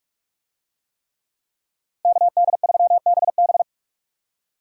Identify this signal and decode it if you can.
Morse code, heard as a keyed tone, about 42 wpm, 700 Hz. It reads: KD3BB